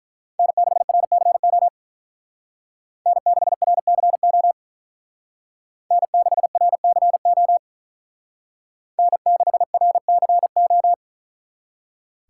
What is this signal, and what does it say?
Morse code, heard as a keyed tone, about 42 wpm, 700 Hz. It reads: N6RCO N6RCO N6RCO N6RCO
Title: Speed Racing - USA Call Signs 35wpm